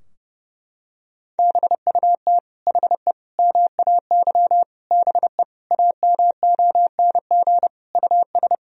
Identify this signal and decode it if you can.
Morse code, heard as a keyed tone, about 30 wpm, 700 Hz. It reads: BUT HE MAY BE AMONG US
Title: Sentences from Top 1000 Words - Review Part 3 - 30wpm